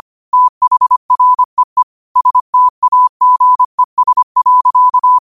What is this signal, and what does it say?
Morse code, heard as a keyed tone, about 25 wpm, 1000 Hz. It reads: THREE STAGES.